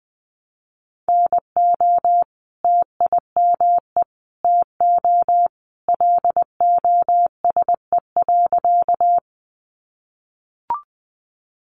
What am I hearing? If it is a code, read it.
Morse code, heard as a keyed tone, about 20 wpm, 700 Hz. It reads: NO TIME TO LOSE.
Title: Sentences from Top 1100 Words 20wpm